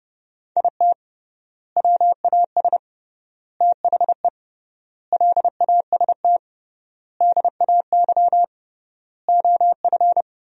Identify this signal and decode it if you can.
Morse code, heard as a keyed tone, about 30 wpm, 700 Hz. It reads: IT WAS THE LAST DAY OF
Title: 3x Word-Spacing - Sentences from Top 300 Words - 30wpm